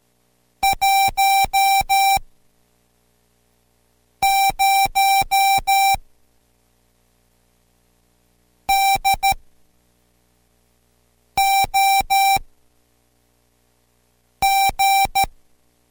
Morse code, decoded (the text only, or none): 10 DOG